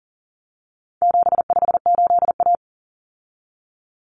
Morse code, heard as a keyed tone, about 40 wpm, 700 Hz. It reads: 758A
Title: Common Cut Numbers - 4-Digit 40wpm